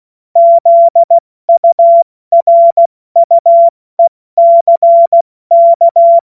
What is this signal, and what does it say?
Morse code, heard as a keyed tone, about 16 wpm, spaced as for 14 wpm, 680 Hz. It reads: ZURUECK
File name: Morsekurs / Witze / Witz 05 Dezember 2023 70 F80